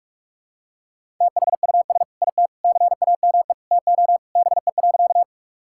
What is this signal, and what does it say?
Morse code, heard as a keyed tone, about 41 wpm, 700 Hz. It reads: THUS IT CAME TO BE.
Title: Sentences from Top 1000 Words - Review Part 5 - 45wpm